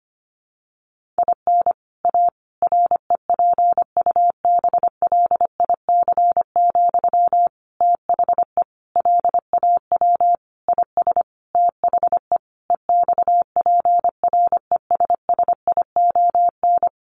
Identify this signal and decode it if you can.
Morse code, heard as a keyed tone, about 25 wpm, 700 Hz. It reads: IN A REPUBLIC, THE LAW IS THE EXPRESSION
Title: Sentences from Top 1500 Words 25wpm